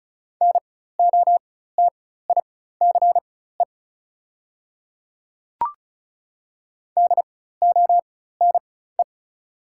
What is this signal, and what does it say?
Morse code, heard as a keyed tone, about 36 wpm, spaced as for 16 wpm, 690 Hz. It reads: NOTICE DONE
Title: 4x Character Spacing - Top 500 Words - 35wpm